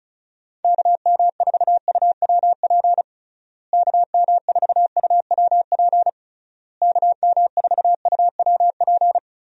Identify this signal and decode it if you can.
Morse code, heard as a keyed tone, about 36 wpm, 700 Hz. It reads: KM4UWP KM4UWP KM4UWP
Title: USA Call Signs - 6 Characters - Repeated 3x - 35wpm